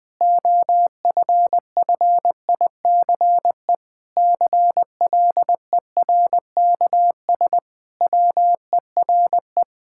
Morse code, heard as a keyed tone, about 20 wpm, 700 Hz. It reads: OFFICE CLERKS WERE
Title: Pangram Challenge v2 - 20wpm